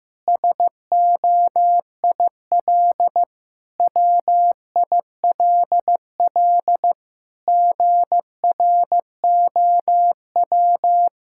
Morse code, heard as a keyed tone, about 15 wpm, 700 Hz. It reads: SOIL WILL GROW
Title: Sentences from Top 1000 Words - Review Part 6 - 15wpm